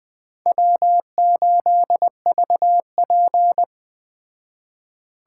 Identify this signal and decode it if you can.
Morse code, heard as a keyed tone, about 20 wpm, 700 Hz. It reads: W8VP
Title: Speed Racing - USA 1x2 Call Signs 17wpm